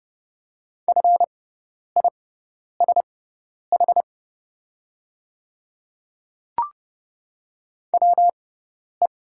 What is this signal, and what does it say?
Morse code, heard as a keyed tone, about 30 wpm, spaced as for 10 wpm, 700 Hz. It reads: FISH WE
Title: 6x Character Spacing - Top 300 Words - 30wpm